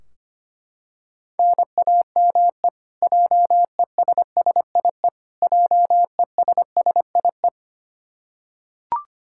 Morse code, heard as a keyed tone, about 25 wpm, 700 Hz. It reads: NAME JESSIE JESSIE
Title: Practice QSO - Exchange 1 - Beginning - 25wpm